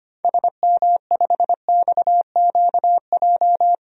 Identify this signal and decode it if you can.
Morse code, heard as a keyed tone, about 25 wpm, 700 Hz. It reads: SM5XQJ
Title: DXCC-100 Call Signs - 6 Characters - 25wpm